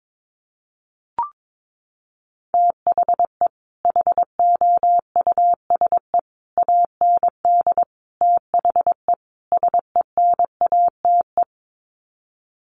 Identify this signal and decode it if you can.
Morse code, heard as a keyed tone, about 22 wpm, 700 Hz. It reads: THE HOUSE AND THE SENATE
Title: Sets of 5 Words Encore2 22wpm